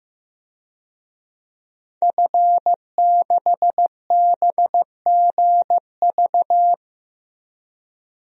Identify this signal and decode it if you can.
Morse code, heard as a keyed tone, about 15 wpm, 700 Hz. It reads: F6BGV
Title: DXCC-100 Call Signs - 5 Characters - 15wpm